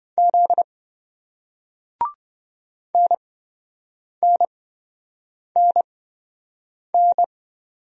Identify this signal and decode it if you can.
Morse code, heard as a keyed tone, about 26 wpm, spaced as for 7 wpm, 700 Hz. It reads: Z NNNN